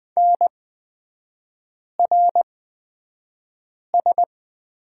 Morse code, heard as a keyed tone, about 20 wpm, spaced as for 5 wpm, 700 Hz. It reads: NRS
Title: 224 Lesson 36e - BT - Three Characters 20wpm